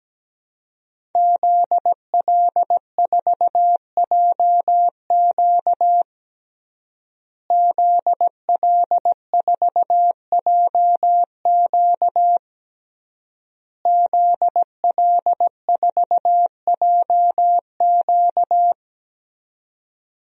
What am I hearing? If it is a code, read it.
Morse code, heard as a keyed tone, about 17 wpm, 700 Hz. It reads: ZL4JQ ZL4JQ ZL4JQ
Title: DXCC-100 Call Signs - 5 Characters - Repeated 3x - 17wpm